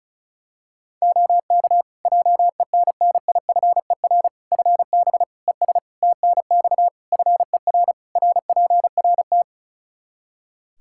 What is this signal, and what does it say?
Morse code, heard as a keyed tone, about 36 wpm, 700 Hz. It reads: OK JENNIFER FB ES TNX FER RPRT
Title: Practice QSO - Exchange 2 - Info - 35wpm